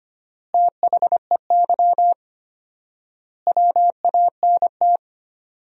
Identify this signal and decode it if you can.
Morse code, heard as a keyed tone, about 25 wpm, 700 Hz. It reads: THEY WANT
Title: 4x Word-Spacing - Sentences from Top 200 Words - 25wpm